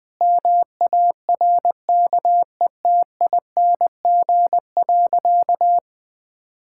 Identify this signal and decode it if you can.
Morse code, heard as a keyed tone, about 21 wpm, 700 Hz. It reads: MARKETING.
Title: Sentences from Top 1000 Words - Review Part 6 - 20wpm